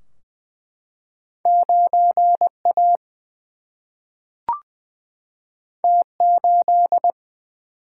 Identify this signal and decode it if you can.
Morse code, heard as a keyed tone, about 20 wpm, 700 Hz. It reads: 9 A T 8
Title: Single-Letter Single-Number 20wpm